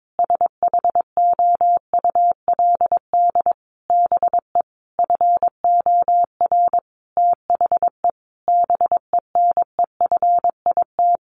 Morse code, heard as a keyed tone, about 22 wpm, 700 Hz. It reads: SHOULD BE FOR THE BENEFIT
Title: Sentences from Top 1300 Words 22wpm